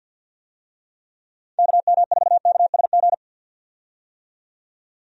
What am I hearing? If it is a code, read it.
Morse code, heard as a keyed tone, about 48 wpm, 700 Hz. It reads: KM4KSG
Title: USA Call Signs - 6 Characters - 50wpm